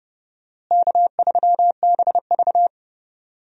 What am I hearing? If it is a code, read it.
Morse code, heard as a keyed tone, about 30 wpm, 700 Hz. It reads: K3BV
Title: USA Call Signs - 4 Characters - Repeated 3x - 30wpm